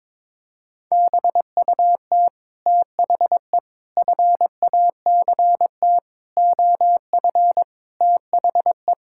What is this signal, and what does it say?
Morse code, heard as a keyed tone, about 22 wpm, 700 Hz. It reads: BUT THE FACT OF THE
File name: Sets of 9 Words - 22wpm